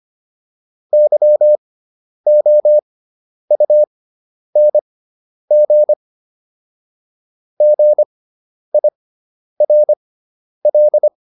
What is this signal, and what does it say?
Morse code, heard as a keyed tone, about 25 wpm, spaced as for 10 wpm, 600 Hz. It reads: YOUNG GIRL